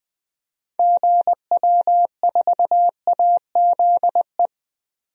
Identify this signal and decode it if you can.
Morse code, heard as a keyed tone, about 20 wpm, 700 Hz. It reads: GW4AZE
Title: DXCC-100 Call Signs - 6 Characters - 20wpm